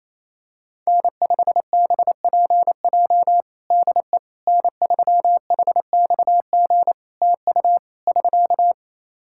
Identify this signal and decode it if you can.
Morse code, heard as a keyed tone, about 28 wpm, 700 Hz. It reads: N5BPJ DE N3HXG TU <SK>
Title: K1USN SST QSO Practice 28wpm